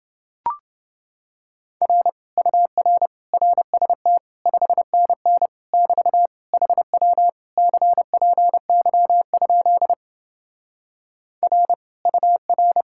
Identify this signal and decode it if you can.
Morse code, heard as a keyed tone, about 29 wpm, 700 Hz. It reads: R UR RST 5NN <BT> HW CPY? R UR